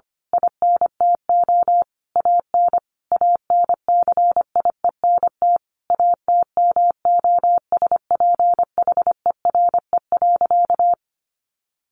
Morse code, heard as a keyed tone, about 25 wpm, 700 Hz. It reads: INTO AN ANCIENT ATMOSPHERE.
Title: A Sound Of Thunder 25wpm